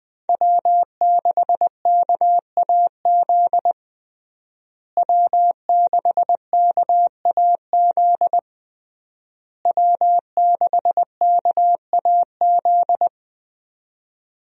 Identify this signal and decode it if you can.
Morse code, heard as a keyed tone, about 21 wpm, 700 Hz. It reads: W6KAZ W6KAZ W6KAZ
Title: USA Call Signs - 5 Characters - Repeated 3x - 20wpm